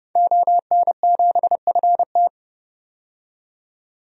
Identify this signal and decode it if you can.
Morse code, heard as a keyed tone, about 30 wpm, 690 Hz. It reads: ON7FT
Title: DXCC-100 Call Signs - 5 Characters - 30wpm